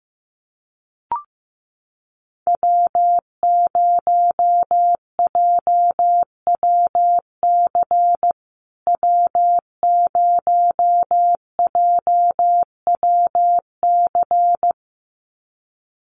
Morse code, heard as a keyed tone, about 15 wpm, 700 Hz. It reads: W0JWC W0JWC
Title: USA Call Signs Repeated 2x 15wpm